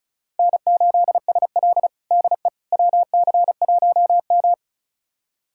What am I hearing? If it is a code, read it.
Morse code, heard as a keyed tone, about 36 wpm, 700 Hz. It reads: N8SL DE WC1M
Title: USA Callsign DE Callsign - 4 Characters - 35wpm